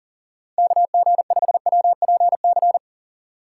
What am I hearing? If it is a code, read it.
Morse code, heard as a keyed tone, about 38 wpm, 700 Hz. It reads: KG5WPC
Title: USA Call Signs - 2x Character Spacing - 40wpm